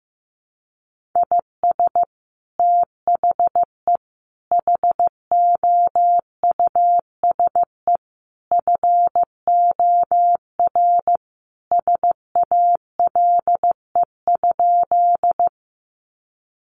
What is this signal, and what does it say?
Morse code, heard as a keyed tone, about 15 wpm, 700 Hz. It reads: IS THE HOUSE FOR SALE?
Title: Sentences from Top 1400 Words 15wpm